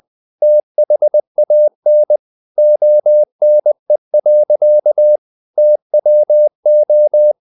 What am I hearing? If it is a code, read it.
Morse code, heard as a keyed tone, about 20 wpm, 600 Hz. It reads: THAN ONE. TWO
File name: CW / Practice / EnglishSayings20